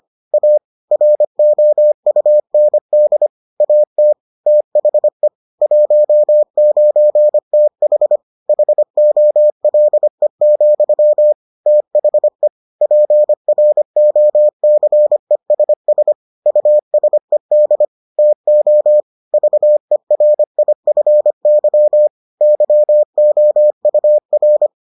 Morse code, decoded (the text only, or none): A ROUND AT THE 19TH HOLE, THE PROCESS USED TO VERIFY YOUR